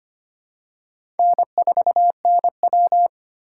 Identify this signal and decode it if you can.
Morse code, heard as a keyed tone, about 25 wpm, 700 Hz. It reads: N4NW
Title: K1USN SST QSO Practice - 25wpm